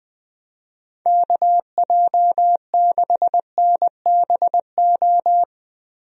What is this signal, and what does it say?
Morse code, heard as a keyed tone, about 20 wpm, 700 Hz. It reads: KJ6NBO
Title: USA Call Signs - 6 Characters - Repeated 2x - 20wpm